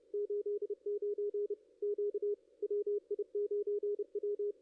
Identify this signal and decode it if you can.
Morse code, heard as a keyed tone, about 32 wpm, 410 Hz, with radio noise. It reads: 89 Q WI9W
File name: contests / ss-cw-k5tr-op-n5rz -2300